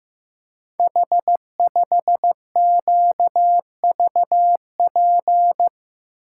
Morse code, heard as a keyed tone, about 15 wpm, 700 Hz. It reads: H5QVP
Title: Code Groups - 5 Letter:Numbers - 15wpm